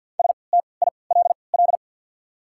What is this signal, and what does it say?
Morse code, heard as a keyed tone, about 52 wpm, 690 Hz. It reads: S T I L L
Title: 3x Character Spacing - Top 300 Words - 50wpm